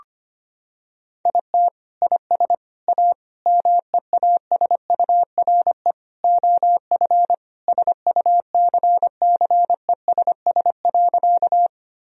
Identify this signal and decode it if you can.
Morse code, heard as a keyed tone, about 25 wpm, 700 Hz. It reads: IT IS A MEASURE OF SUCCESS.
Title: Sentences from Top 1000 Words - Review Part 2 - 25wpm